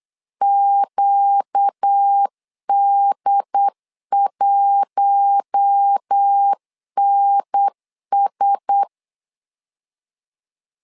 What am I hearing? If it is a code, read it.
Morse code, heard as a keyed tone, about 8 wpm, 800 Hz. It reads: QD1NS
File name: LEZIONI / 4 Q1S 8wpm 2710